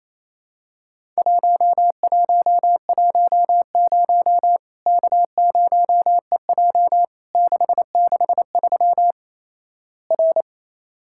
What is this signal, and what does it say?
Morse code, heard as a keyed tone, about 28 wpm, 700 Hz. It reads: 1110 K0EJ 663 R
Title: Worked All Europe DX Contest - QTC Exchange - 28wpm